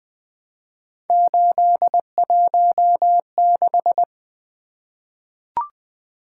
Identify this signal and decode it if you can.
Morse code, heard as a keyed tone, about 20 wpm, 700 Hz. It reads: 816
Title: Three Letter-Number Combinations - 20wpm